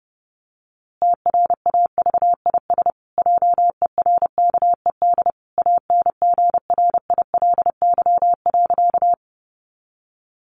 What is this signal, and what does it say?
Morse code, heard as a keyed tone, about 30 wpm, 700 Hz. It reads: TRAVIS JERKED ANGRILY.
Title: A Sound Of Thunder 30wpm